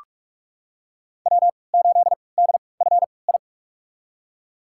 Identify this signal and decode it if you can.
Morse code, heard as a keyed tone, about 41 wpm, spaced as for 25 wpm, 700 Hz. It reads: W8DFI
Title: USA Call Signs - 3x Character Spacing - 45wpm